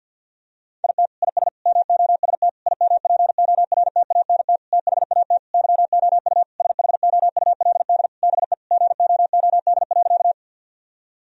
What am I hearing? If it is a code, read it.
Morse code, heard as a keyed tone, about 53 wpm, 700 Hz. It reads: IT IS MOST IMPORTANT THAT YOU SHOULD BE GOOD.
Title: Sentences from Top 1000 Words - Review Part 1 - 50wpm